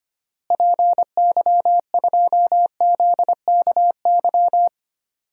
Morse code, heard as a keyed tone, about 25 wpm, 700 Hz. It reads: PY2ZKY
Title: DXCC-100 Call Signs - 6 Characters - Repeated 3x - 25wpm